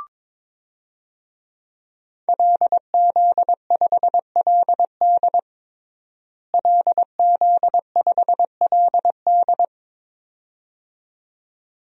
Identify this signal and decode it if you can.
Morse code, heard as a keyed tone, about 22 wpm, 700 Hz. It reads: LZ5LD LZ5LD
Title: DXCC-100 Call Signs - 5 Characters - Repeated 2x - 22wpm